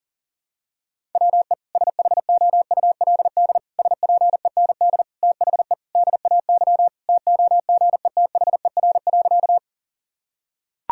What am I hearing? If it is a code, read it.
Morse code, heard as a keyed tone, about 37 wpm, 700 Hz. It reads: WE SHOULD SPEND THE DAY TOGETHER.
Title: Sentences from Top 1000 Words - Review Part 5 - 40wpm